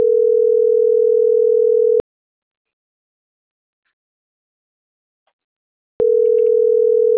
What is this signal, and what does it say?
Ringback tone (North America), 440 + 480 Hz.